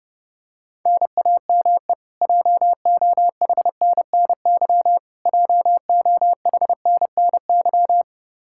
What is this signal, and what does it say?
Morse code, heard as a keyed tone, about 30 wpm, 700 Hz. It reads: NAME JOHNNY JOHNNY
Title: Practice QSO - Exchange 1 - Beginning - 30wpm